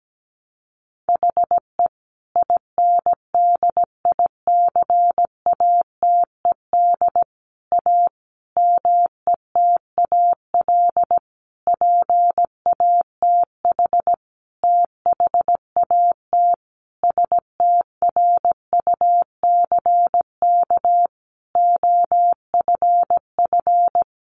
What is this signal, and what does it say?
Morse code, heard as a keyed tone, about 17 wpm, 700 Hz. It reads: HE INDICATED A METAL PATH THAT STRUCK OFF
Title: A Sound Of Thunder 17wpm